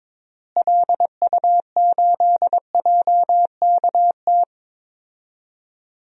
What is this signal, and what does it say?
Morse code, heard as a keyed tone, about 22 wpm, 700 Hz. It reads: LU8JKT
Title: DXCC-100 Call Signs - 6 Characters - Repeated 3x - 22wpm